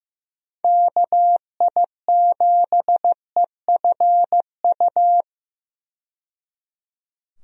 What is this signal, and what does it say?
Morse code, heard as a keyed tone, about 15 wpm, 700 Hz. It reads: KI7EFU
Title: USA Call Signs - 6 Characters - 15wpm